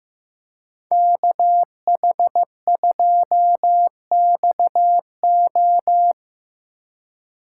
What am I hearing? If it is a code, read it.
Morse code, heard as a keyed tone, about 17 wpm, 700 Hz. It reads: KH2XO
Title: Speed Racing - USA 2x2 Call Signs 15wpm